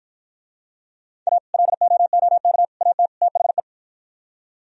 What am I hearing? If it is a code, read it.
Morse code, heard as a keyed tone, about 50 wpm, 700 Hz. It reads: A LOOK AT THE